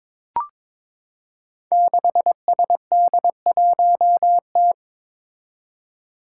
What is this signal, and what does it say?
Morse code, heard as a keyed tone, about 22 wpm, 700 Hz. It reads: 6SD1T
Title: Five Challenging Letter-Number Combinations 22wpm